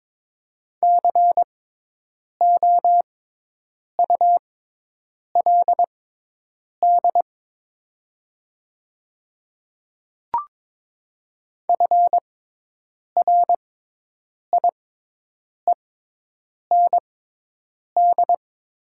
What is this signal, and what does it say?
Morse code, heard as a keyed tone, about 22 wpm, spaced as for 8 wpm, 700 Hz. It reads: COULD FRIEND
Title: 6x Character Spacing - Top 300 Words - 22wpm